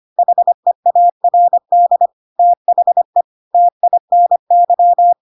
Morse code, heard as a keyed tone, about 25 wpm, 700 Hz. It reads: HEARD THE TINY
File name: Pinocchio / Pinocchio ch1 25 25 ik1hge